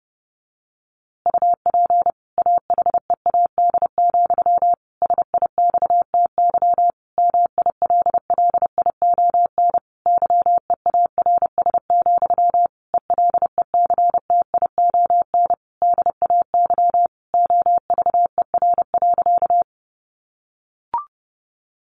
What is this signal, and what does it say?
Morse code, heard as a keyed tone, about 30 wpm, 700 Hz. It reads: UP AHEAD, SIXTY MILLION YEARS, ELECTION DAY OVER.
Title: A Sound Of Thunder 30wpm